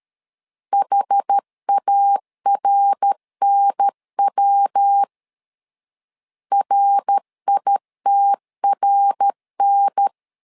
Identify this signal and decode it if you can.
Morse code, heard as a keyed tone, about 13 wpm, 800 Hz. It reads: HARNW RITRN
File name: LEZIONI / 4 Q 12wpm 1833